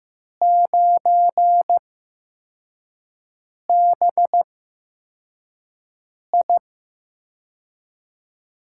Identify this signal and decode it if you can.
Morse code, heard as a keyed tone, about 15 wpm, spaced as for 4 wpm, 700 Hz. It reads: 9BI
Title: USA Call Signs - 4 Characters - 8x Character Spacing - 15wpm